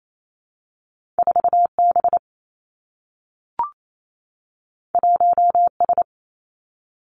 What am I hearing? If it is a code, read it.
Morse code, heard as a keyed tone, about 28 wpm, 700 Hz. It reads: 4 B 1 S
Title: Two Challenging Letter-Number Combinations 28wpm